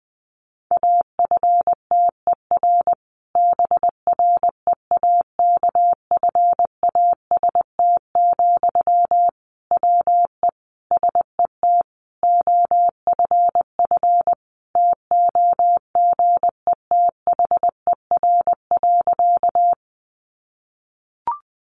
Morse code, heard as a keyed tone, about 20 wpm, 700 Hz. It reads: AFTER BREAKFAST, WE SET OFF TOGETHER.
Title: Sentences from Top 1500 Words 20wpm